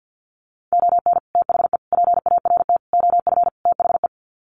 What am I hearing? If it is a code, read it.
Morse code, heard as a keyed tone, about 50 wpm, 700 Hz. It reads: ON THE PART OF THE
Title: Sets of 5 Words 50wpm